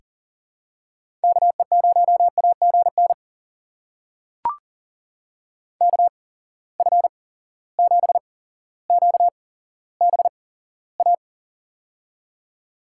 Morse code, heard as a keyed tone, about 40 wpm, spaced as for 17 wpm, 700 Hz. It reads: KE0AGN KF7QBA
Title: USA Call Signs - 8x Character Spacing - 40wpm